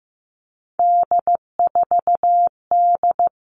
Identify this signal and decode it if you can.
Morse code, heard as a keyed tone, about 15 wpm, 700 Hz. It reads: D4D
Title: All Cut Numbers - 3-Digit 15wpm